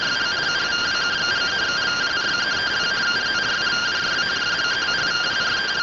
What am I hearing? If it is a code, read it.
Two-tone teleprinter signal, about 170 Hz shift.